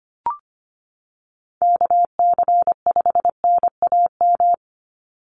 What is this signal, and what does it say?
Morse code, heard as a keyed tone, about 25 wpm, 700 Hz. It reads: KC5NAM
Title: USA Call Signs 25wpm